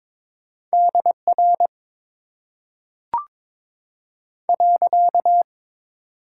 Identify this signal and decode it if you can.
Morse code, heard as a keyed tone, about 22 wpm, 700 Hz. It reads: DR .
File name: Top 90 QSO Elements - 6x Character Spacing - 22wpm